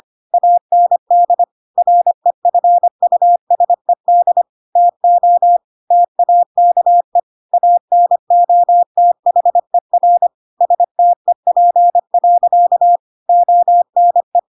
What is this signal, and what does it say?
Morse code, heard as a keyed tone, about 25 wpm, 700 Hz. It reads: AND REFUSED TO TAKE ANOTHER STEP. ONE
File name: Pinocchio / Pinocchio ch3 25 25 ik1hge